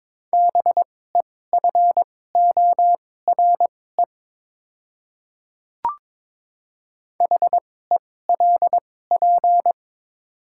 Morse code, heard as a keyed tone, about 22 wpm, spaced as for 16 wpm, 700 Hz. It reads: BEFORE HELP
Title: 2x Character Spacing - Top 300 Words - 22wpm